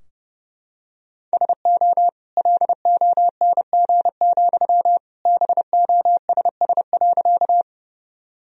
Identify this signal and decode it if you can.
Morse code, heard as a keyed tone, about 30 wpm, 700 Hz. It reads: SO LONG, BOSS.
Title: Sentences from Top 1000 Words - Review Part 4 - 30wpm